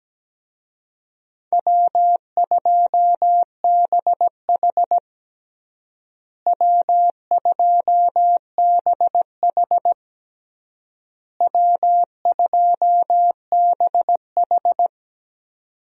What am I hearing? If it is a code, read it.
Morse code, heard as a keyed tone, about 17 wpm, 700 Hz. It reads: W2BH W2BH W2BH
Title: USA Call Signs - 4 Characters - Repeated 3x - 17wpm